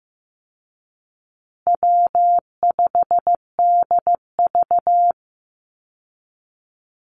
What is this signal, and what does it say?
Morse code, heard as a keyed tone, about 15 wpm, 700 Hz. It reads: W5DV
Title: Easy Call Signs Repeated 3x - 15wpm